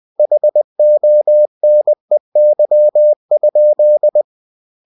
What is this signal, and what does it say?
Morse code, heard as a keyed tone, about 20 wpm, 600 Hz. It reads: HONEY?
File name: cw-books / winnie-the-pooh / winnie-20wpm-page-0003